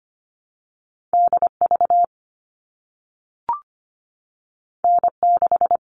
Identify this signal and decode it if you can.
Morse code, heard as a keyed tone, about 25 wpm, 700 Hz. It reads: DV N6
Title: All Cut Numbers - 2-Digit 25wpm